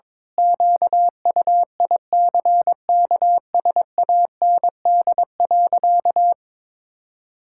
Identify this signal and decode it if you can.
Morse code, heard as a keyed tone, about 22 wpm, 700 Hz. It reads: QUICKSAND.
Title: Pangram Challenge v2 - 22wpm